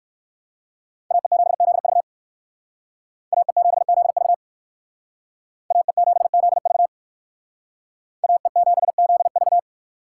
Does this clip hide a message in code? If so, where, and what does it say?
Yes: Morse code from 1.1 to 9.6 s, heard as a keyed tone, about 57 wpm, 690 Hz. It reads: AE7ZV AE7ZV AE7ZV AE7ZV